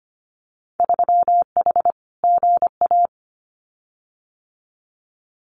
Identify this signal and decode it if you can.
Morse code, heard as a keyed tone, about 25 wpm, 700 Hz. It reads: 3H GA
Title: Winter Field Day Exchanges Encore 25wpm